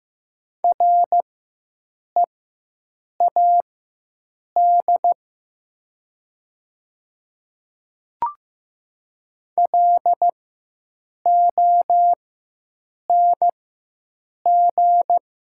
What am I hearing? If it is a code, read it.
Morse code, heard as a keyed tone, about 15 wpm, spaced as for 7 wpm, 700 Hz. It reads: READ LONG
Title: 4x Character Spacing - Top 200 Words - 15wpm